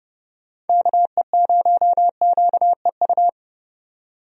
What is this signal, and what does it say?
Morse code, heard as a keyed tone, about 30 wpm, 700 Hz. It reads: KE0QEU
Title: USA Call Signs - 6 Characters - 30wpm